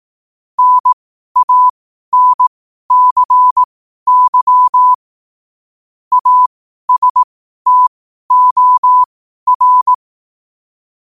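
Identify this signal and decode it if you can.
Morse code, heard as a keyed tone, about 18 wpm, spaced as for 12 wpm, 1000 Hz. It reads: NANCY ASTOR
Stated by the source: 12 wpm Morse.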